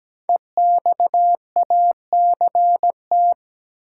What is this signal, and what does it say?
Morse code, heard as a keyed tone, about 17 wpm, 700 Hz. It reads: EXACT
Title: Sentences from Top 1300 Words - New Word Drill - 17wpm